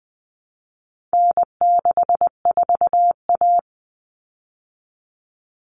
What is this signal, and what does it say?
Morse code, heard as a keyed tone, about 20 wpm, 700 Hz. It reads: N64A
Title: Common Cut Numbers - 4-Digit 20wpm